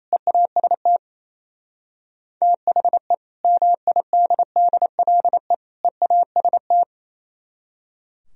Morse code, heard as a keyed tone, about 37 wpm, 700 Hz. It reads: EAST THE MIDDLE EAST